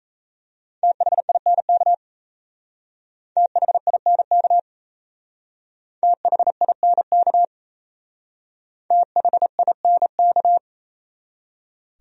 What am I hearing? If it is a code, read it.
Morse code, heard as a keyed tone, about 34 wpm, 700 Hz. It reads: THINK THINK THINK THINK
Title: Speed-Racing Top 200 Words 28wpm